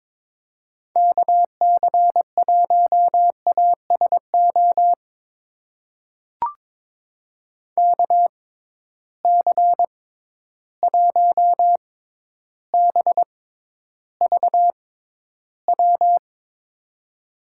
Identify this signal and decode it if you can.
Morse code, heard as a keyed tone, about 22 wpm, spaced as for 11 wpm, 700 Hz. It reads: KC1ASO KC1BVW
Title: USA Call Signs - 6 Characters - 6x Character Spacing - 22wpm